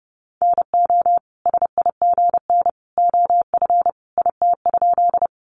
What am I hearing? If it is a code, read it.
Morse code, heard as a keyed tone, about 30 wpm, 700 Hz. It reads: NO SIGN OF IT?
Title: Sentences from Top 1000 Words 30wpm